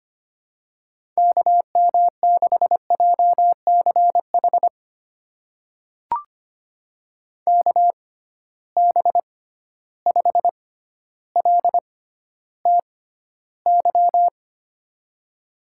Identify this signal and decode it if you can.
Morse code, heard as a keyed tone, about 25 wpm, spaced as for 13 wpm, 700 Hz. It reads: KM6JCH KB5LTY
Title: USA Call Signs - 6 Characters - 6x Character Spacing - 25wpm